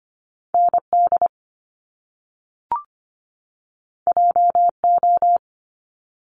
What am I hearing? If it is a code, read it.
Morse code, heard as a keyed tone, about 25 wpm, 700 Hz. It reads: ND JO
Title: Top Two Letter Combinations in Words 25wpm